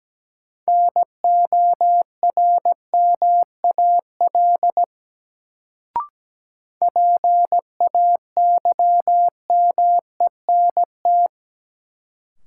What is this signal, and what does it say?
Morse code, heard as a keyed tone, about 17 wpm, 700 Hz. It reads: NORMAL PAYMENT
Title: Sentences from Top 1200 Words - New Word Drill Alt - 17wpm